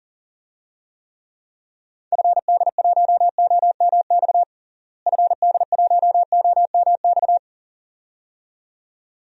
Morse code, heard as a keyed tone, about 39 wpm, 700 Hz. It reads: FD1OMX FD1OMX
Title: DXCC-100 Call Signs - 6 Characters - Repeated 2x - 40wpm